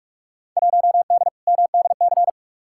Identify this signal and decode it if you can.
Morse code, heard as a keyed tone, about 46 wpm, 700 Hz. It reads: 1D MDC
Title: Field Day Exchanges - 45wpm